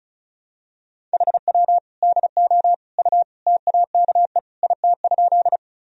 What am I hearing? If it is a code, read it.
Morse code, heard as a keyed tone, about 36 wpm, 700 Hz. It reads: HW DO U TAKE IT?
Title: Sentences from Top 200 Words and Abbreviations 35wpm